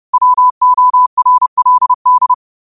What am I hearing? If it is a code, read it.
Morse code, heard as a keyed tone, about 30 wpm, 1000 Hz. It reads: WORLD
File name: qotdfiles / 30wpm